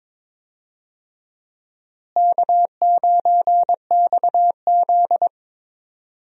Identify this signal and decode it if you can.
Morse code, heard as a keyed tone, about 21 wpm, 700 Hz. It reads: K9XZ
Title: USA Call Signs - 4 Characters - Repeated 3x - 22wpm